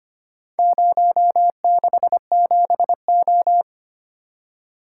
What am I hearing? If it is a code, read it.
Morse code, heard as a keyed tone, about 25 wpm, 700 Hz. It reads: 067O
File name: practice / 25wpm / Four Letter-Number Combinations - 25wpm